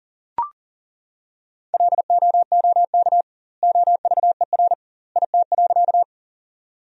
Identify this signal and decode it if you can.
Morse code, heard as a keyed tone, about 40 wpm, 700 Hz. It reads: LOOK OVER IT.
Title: 2x Word-Spacing - Sentences from Top 100 Words - 40wpm